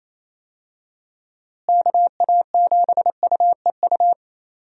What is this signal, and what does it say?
Morse code, heard as a keyed tone, about 28 wpm, 700 Hz. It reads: KA7UEU
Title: USA Call Signs - 6 Characters - Repeated 2x - 28wpm